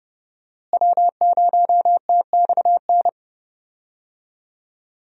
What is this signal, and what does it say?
Morse code, heard as a keyed tone, about 30 wpm, 700 Hz. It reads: W0TXN
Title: USA Call Signs - 5 Characters - Repeated 3x - 30wpm